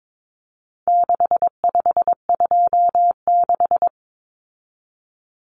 Morse code, heard as a keyed tone, about 22 wpm, 700 Hz. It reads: 6526
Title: Numbers - 4-Digit 22wpm